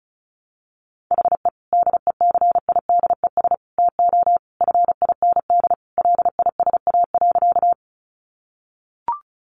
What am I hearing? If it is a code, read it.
Morse code, heard as a keyed tone, about 34 wpm, 700 Hz. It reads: HE DECIDES TO FIND LISA.
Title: Additional Easy Phrases 35wpm